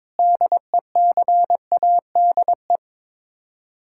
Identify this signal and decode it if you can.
Morse code, heard as a keyed tone, about 22 wpm, 700 Hz. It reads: DECADE
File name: Sets of 9 Words - 22wpm